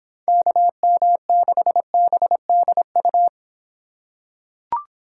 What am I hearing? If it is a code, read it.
Morse code, heard as a keyed tone, about 26 wpm, 700 Hz. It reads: KM6BDU